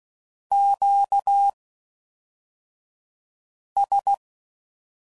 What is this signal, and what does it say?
Morse code, heard as a keyed tone, about 16 wpm, spaced as for 4 wpm, 800 Hz. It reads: QS